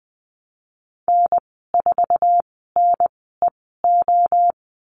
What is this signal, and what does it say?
Morse code, heard as a keyed tone, about 20 wpm, spaced as for 15 wpm, 700 Hz. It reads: N4NEO
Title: 069 Lesson 12g - Letter d - Call Signs 20wpm